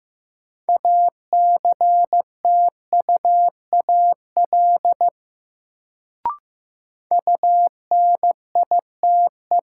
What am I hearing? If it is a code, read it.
Morse code, heard as a keyed tone, about 15 wpm, 700 Hz. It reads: ACTUAL UNITE
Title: Sentences from Top 1000 Words - Review Part 5 - New Word Drill - 15wpm